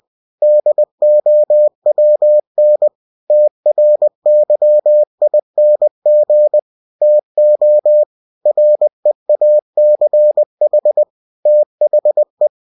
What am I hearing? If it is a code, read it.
Morse code, heard as a keyed tone, about 20 wpm, 600 Hz. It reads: DOWN TRYING TO REACH THE
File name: cw-books / winnie-the-pooh / winnie-20wpm-page-0048